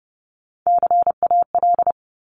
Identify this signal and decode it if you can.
Morse code, heard as a keyed tone, about 31 wpm, 700 Hz. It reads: CAL
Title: Top Three Letter Combinations in Words 30wpm